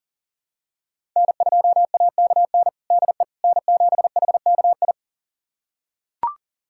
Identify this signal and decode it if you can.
Morse code, heard as a keyed tone, about 40 wpm, 700 Hz. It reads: N2AKN DE N7HKI
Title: USA Callsign DE Callsign - 5 Characters - 40wpm